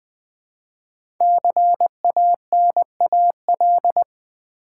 Morse code, heard as a keyed tone, about 20 wpm, 700 Hz. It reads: CANAL
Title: Build A Word - 1 Letter At A Time - 5-Letter Words - 20wpm